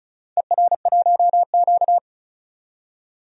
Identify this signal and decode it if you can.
Morse code, heard as a keyed tone, about 36 wpm, 700 Hz. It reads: ER1Q
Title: DXCC-100 Call Signs - 4 Characters - 35wpm